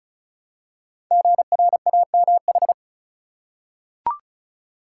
Morse code, heard as a keyed tone, about 36 wpm, 690 Hz. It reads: GRAMH
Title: Code Groups - 5 Letters - 35wpm